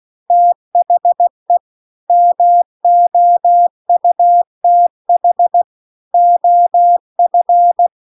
Morse code, heard as a keyed tone, about 16 wpm, 700 Hz. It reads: THE MOUTH OF
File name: CW-The-Big-Twoheaded-River-2-16WPM